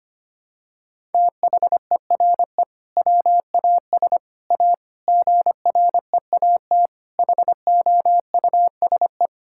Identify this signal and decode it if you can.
Morse code, heard as a keyed tone, about 25 wpm, 700 Hz. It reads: THERE WAS A GREAT HOUSE
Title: Sentences from Top 1600 Words 25wpm